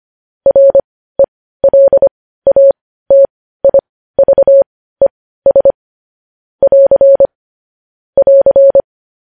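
Morse code, heard as a keyed tone, about 25 wpm, spaced as for 15 wpm, 550 Hz. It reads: RELATIVES <AR> <AR>
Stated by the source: Morse code news headlines at 15 WPM.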